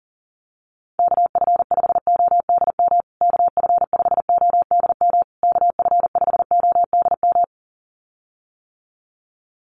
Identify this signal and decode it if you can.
Morse code, heard as a keyed tone, about 40 wpm, 700 Hz. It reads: KF5ODM KF5ODM KF5ODM
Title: USA Call Signs Repeated 3x 40wpm